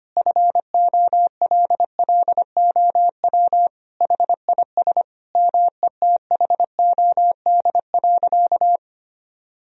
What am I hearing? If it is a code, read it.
Morse code, heard as a keyed tone, about 25 wpm, 700 Hz. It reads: FOLLOW HIS METHOD.
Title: Sentences from Top 1000 Words - Review Part 3 - 25wpm